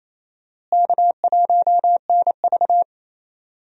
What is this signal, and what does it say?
Morse code, heard as a keyed tone, about 28 wpm, 700 Hz. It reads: K1NV
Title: USA Call Signs - 4 Characters - Repeated 3x - 28wpm